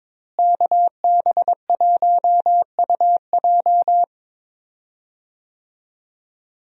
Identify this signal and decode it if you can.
Morse code, heard as a keyed tone, about 22 wpm, 700 Hz. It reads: KB1UJ
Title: USA Call Signs - 5 Characters - Repeated 3x - 22wpm